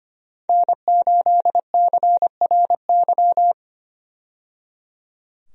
Morse code, heard as a keyed tone, about 25 wpm, 700 Hz. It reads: N8CRY
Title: USA Call Signs - 5 Characters - 25wpm